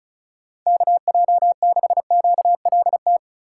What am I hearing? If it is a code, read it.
Morse code, heard as a keyed tone, about 36 wpm, 700 Hz. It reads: KJ6QLT
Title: USA Call Signs - 6 Characters - 4x Character Spacing - 35wpm